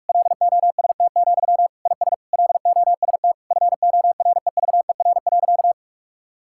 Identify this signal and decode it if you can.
Morse code, heard as a keyed tone, about 46 wpm, 700 Hz. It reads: LOST, IS LOST FOREVER.
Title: Sentences from Top 1600 Words 45wpm